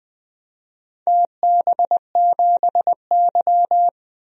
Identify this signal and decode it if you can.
Morse code, heard as a keyed tone, about 20 wpm, 700 Hz. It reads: TB7Y
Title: DXCC-100 Call Signs - 4 Characters - 20wpm